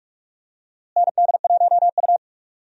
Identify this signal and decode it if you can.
Morse code, heard as a keyed tone, about 46 wpm, 700 Hz. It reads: ND1U
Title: USA Call Signs - 4 Characters - Repeated 2x - 45wpm